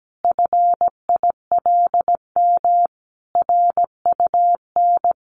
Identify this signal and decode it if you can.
Morse code, heard as a keyed tone, about 17 wpm, 700 Hz. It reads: FILM RUN
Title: A Sound Of Thunder 17wpm